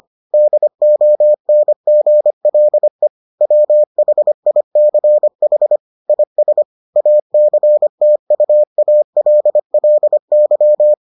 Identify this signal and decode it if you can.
Morse code, heard as a keyed tone, about 25 wpm, 600 Hz. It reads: DONGLE WHICH IS ACTUALLY